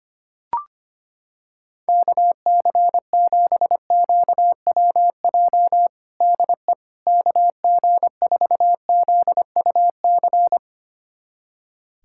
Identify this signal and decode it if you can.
Morse code, heard as a keyed tone, about 25 wpm, 700 Hz. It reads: KC7QWJ DE KG4ZUC
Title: USA Callsign DE Callsign - 6 Characters - 25wpm